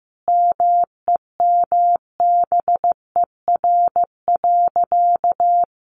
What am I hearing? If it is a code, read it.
Morse code, heard as a keyed tone, about 15 wpm, 700 Hz. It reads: MEMBER.
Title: Sentences from Top 1200 Words 15wpm